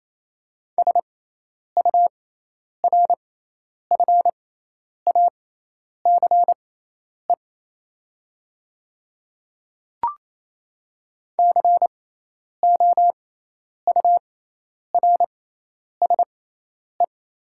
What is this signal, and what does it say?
Morse code, heard as a keyed tone, about 27 wpm, spaced as for 10 wpm, 700 Hz. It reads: SURFACE COURSE
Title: 6x Character Spacing - Top 500 Words - 28wpm